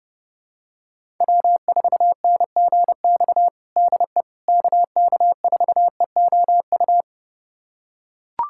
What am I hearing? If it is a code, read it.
Morse code, heard as a keyed tone, about 30 wpm, 700 Hz. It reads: W4NGX DE KK4EOU
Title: USA Callsign DE Callsign 30wpm